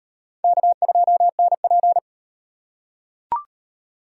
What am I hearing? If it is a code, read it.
Morse code, heard as a keyed tone, about 38 wpm, 700 Hz. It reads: K2NP